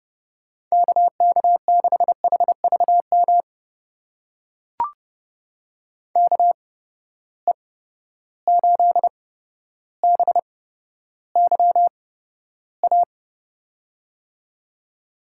Morse code, heard as a keyed tone, about 30 wpm, spaced as for 13 wpm, 700 Hz. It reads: KK6HVM KE8BYA
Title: USA Call Signs - 6 Characters - 8x Character Spacing - 30wpm